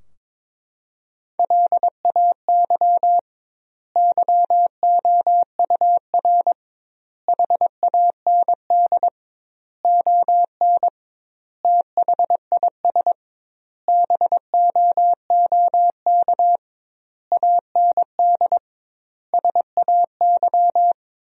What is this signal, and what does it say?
Morse code, heard as a keyed tone, about 22 wpm, 700 Hz. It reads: LAY YOUR HAND ON THIS BOOK AND SAY
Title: 2x Word-Spacing - Sentences from Top 400 Words - 22wpm